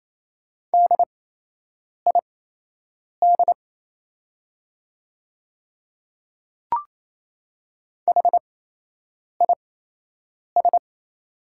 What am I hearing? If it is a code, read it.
Morse code, heard as a keyed tone, about 28 wpm, spaced as for 8 wpm, 700 Hz. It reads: DID HIS
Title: 8x Character Spacing - Top 100 Words - 28wpm